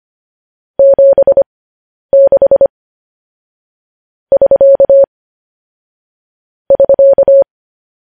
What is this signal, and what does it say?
Morse code, heard as a keyed tone, about 25 wpm, spaced as for 10 wpm, 550 Hz. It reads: 76 <SK> <SK>
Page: news-headlines-in-morse-code-at-10-wpm